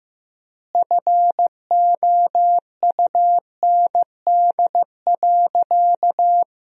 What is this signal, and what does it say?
Morse code, heard as a keyed tone, about 15 wpm, 700 Hz. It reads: FOUND.
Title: 3x Word-Spacing - Sentences from Top 300 Words - 15wpm